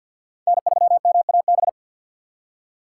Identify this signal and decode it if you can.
Morse code, heard as a keyed tone, about 48 wpm, 700 Hz. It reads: N3MAB
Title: USA Call Signs - 5 Characters - 2x Character Spacing - 50wpm